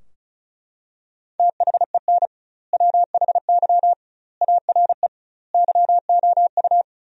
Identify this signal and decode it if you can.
Morse code, heard as a keyed tone, about 36 wpm, 700 Hz. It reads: THEN WHY ARE YOU
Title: 2x Word-Spacing - Sentences from Top 200 Words - 35wpm